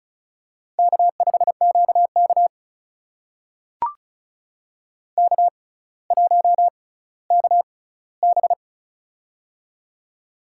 Morse code, heard as a keyed tone, about 36 wpm, spaced as for 18 wpm, 700 Hz. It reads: K5QK K1KB
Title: USA Call Signs - 4 Characters - 6x Character Spacing - 35wpm